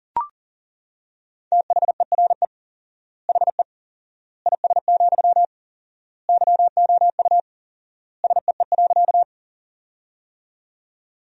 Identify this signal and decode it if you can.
Morse code, heard as a keyed tone, about 37 wpm, 700 Hz. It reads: THERE HE IS, YOU SEE.
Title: 4x Word-Spacing - Sentences from Top 100 Words - 40wpm